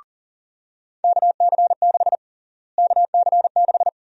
Morse code, heard as a keyed tone, about 40 wpm, 700 Hz. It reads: KC6 KC6
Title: USA Call Sign Prefixes - Repeated 2x - 40wpm